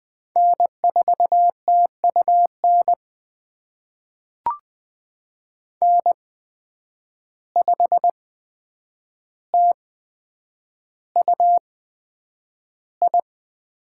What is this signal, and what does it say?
Morse code, heard as a keyed tone, about 20 wpm, spaced as for 9 wpm, 700 Hz. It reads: N4TUN N5TUI
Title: USA Call Signs - 5 Characters - 8x Character Spacing - 20wpm